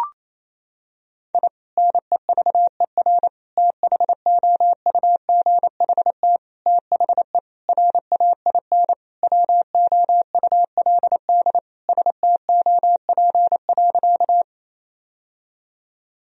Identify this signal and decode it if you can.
Morse code, heard as a keyed tone, about 28 wpm, 700 Hz. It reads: I NEVER THOUGHT THE RAIN WOULD STOP.
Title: Sentences from Top 1000 Words - Review Part 1 - 28wpm